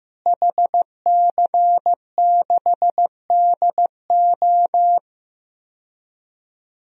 Morse code, heard as a keyed tone, about 15 wpm, 700 Hz. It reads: HC6DO
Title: DXCC-100 Call Signs - 5 Characters - Repeated 3x - 15wpm